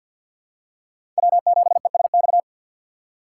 Morse code, heard as a keyed tone, about 53 wpm, 700 Hz. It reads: W7ESX
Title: USA Call Signs - 5 Characters - 3x Character Spacing - 50wpm